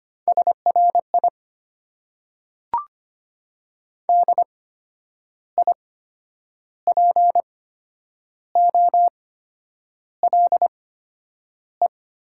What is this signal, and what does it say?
Morse code, heard as a keyed tone, about 25 wpm, spaced as for 9 wpm, 700 Hz. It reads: SRI DIPOLE
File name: Top 90 QSO Elements - 8x Character Spacing - 25wpm